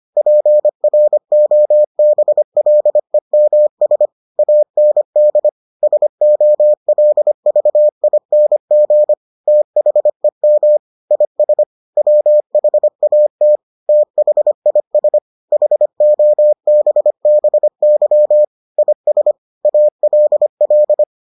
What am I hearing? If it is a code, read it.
Morse code, heard as a keyed tone, about 25 wpm, 600 Hz. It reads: PROBLEMS AND SOLVING THEM IS WHAT THIS HOBBY IS ALL